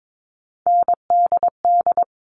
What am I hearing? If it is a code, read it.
Morse code, heard as a keyed tone, about 22 wpm, 700 Hz. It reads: NDD
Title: All Cut Numbers - 3-Digit 22wpm